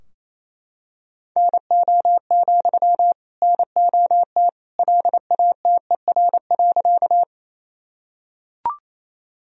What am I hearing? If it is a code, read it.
Morse code, heard as a keyed tone, about 28 wpm, 700 Hz. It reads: NO, NOT LATER.
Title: Sentences from Top 1000 Words - Review Part 4 - 28wpm